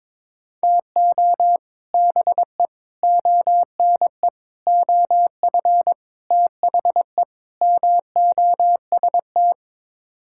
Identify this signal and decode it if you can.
Morse code, heard as a keyed tone, about 21 wpm, 700 Hz. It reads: TO BE ONE OF THE MOST
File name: practice / 22wpm / Sets of 6 Words - 22wpm